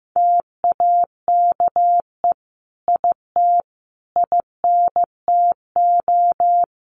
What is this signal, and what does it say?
Morse code, heard as a keyed tone, about 15 wpm, 700 Hz. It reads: TAKE IT INTO
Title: Sentences from Top 1500 Words 15wpm